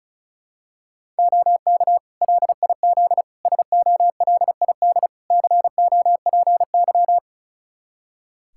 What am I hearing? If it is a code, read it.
Morse code, heard as a keyed tone, about 34 wpm, 700 Hz. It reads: OK LIZ SOLID COPY
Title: K1USN SST QSO Practice 35wpm